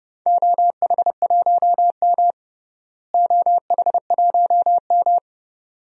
Morse code, heard as a keyed tone, about 30 wpm, 700 Hz. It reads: OH1M OH1M
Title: DXCC-100 Call Signs - 4 Characters - Repeated 2x - 30wpm